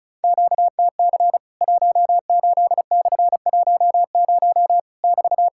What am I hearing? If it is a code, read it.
Morse code, heard as a keyed tone, about 36 wpm, 700 Hz. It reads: QTC 18/10 <BT>
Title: Worked All Europe DX Contest - QTC Exchange - 35wpm